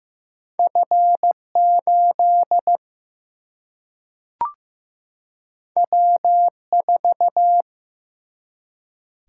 Morse code, heard as a keyed tone, about 15 wpm, 700 Hz. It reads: F8 W4
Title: Single-Letter Single-Number 15wpm